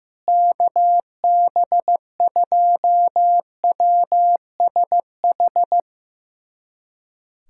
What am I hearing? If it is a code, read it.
Morse code, heard as a keyed tone, about 15 wpm, 700 Hz. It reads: KB2WSH
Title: USA Callsign DE Callsign 15wpm